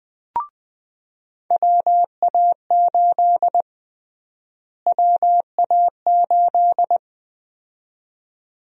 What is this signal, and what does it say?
Morse code, heard as a keyed tone, about 20 wpm, 700 Hz. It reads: WA8 WA8
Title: USA Call Sign Prefixes - Repeated 2x - 20wpm